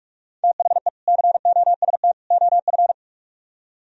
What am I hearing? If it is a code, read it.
Morse code, heard as a keyed tone, about 44 wpm, 700 Hz. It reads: THE COST OF